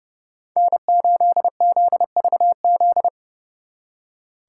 Morse code, heard as a keyed tone, about 39 wpm, 700 Hz. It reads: N8ZVZ